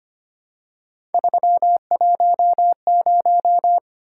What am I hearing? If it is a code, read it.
Morse code, heard as a keyed tone, about 25 wpm, 700 Hz. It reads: 310
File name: Three Letter-Number Combinations - 25wpm